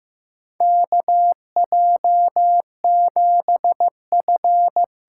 Morse code, heard as a keyed tone, about 15 wpm, 700 Hz. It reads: KJ7F
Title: USA Call Signs - 6 Characters - 6x Character Spacing - 15wpm